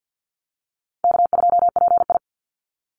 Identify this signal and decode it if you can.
Morse code, heard as a keyed tone, about 50 wpm, 700 Hz. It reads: K2PI
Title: Call Signs - 50wpm